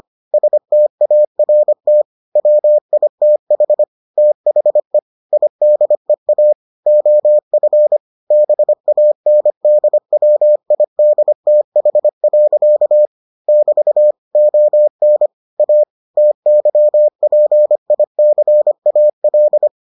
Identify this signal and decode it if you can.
Morse code, heard as a keyed tone, about 25 wpm, 600 Hz. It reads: START WITH THE IDEA OF BANDWIDTH. <BT> ON A TYPICAL